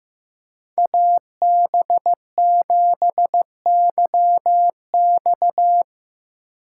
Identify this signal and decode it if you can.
Morse code, heard as a keyed tone, about 17 wpm, 700 Hz. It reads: AB7YX
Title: Speed Racing - USA 2x2 Call Signs 15wpm